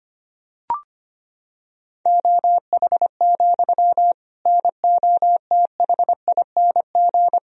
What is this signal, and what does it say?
Morse code, heard as a keyed tone, about 25 wpm, 700 Hz. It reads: OH, NOTHING
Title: Sentences from Top 1000 Words - Review Part 3 - 25wpm